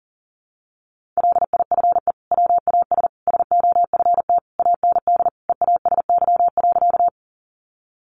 Morse code, heard as a keyed tone, about 40 wpm, 700 Hz. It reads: LIFE WAS SOFT AND EASY.
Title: Sentences from Top 1500 Words 40wpm